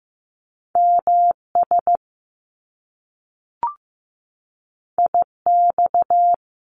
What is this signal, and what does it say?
Morse code, heard as a keyed tone, about 15 wpm, 700 Hz. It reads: MS IX
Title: Top Two Letter Combinations in Words 15wpm